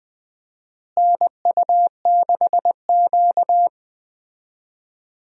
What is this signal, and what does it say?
Morse code, heard as a keyed tone, about 20 wpm, 700 Hz. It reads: NU6Q
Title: USA Call Signs - 4 Characters - Repeated 3x - 20wpm